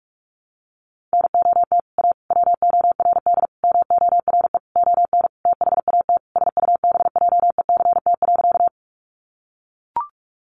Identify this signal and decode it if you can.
Morse code, heard as a keyed tone, about 44 wpm, 700 Hz. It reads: NOT A WORD MORE ON THAT SUBJECT.
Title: Sentences from Top 800 Words 45wpm